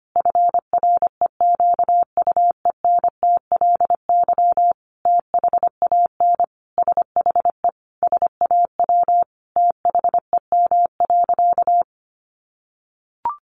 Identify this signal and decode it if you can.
Morse code, heard as a keyed tone, about 25 wpm, 700 Hz. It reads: FREQUENTLY THAN SHE SAW THEM.
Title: Sentences from Top 1300 Words 25wpm